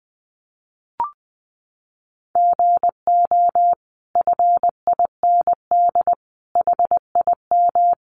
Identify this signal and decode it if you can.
Morse code, heard as a keyed tone, about 20 wpm, 700 Hz. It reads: GO FIND HIM
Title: Sentences from Top 1100 Words 20wpm